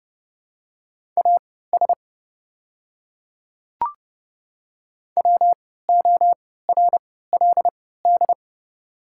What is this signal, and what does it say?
Morse code, heard as a keyed tone, about 30 wpm, spaced as for 17 wpm, 700 Hz. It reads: AS WORLD
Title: 3x Character Spacing - Top 200 Words - 30wpm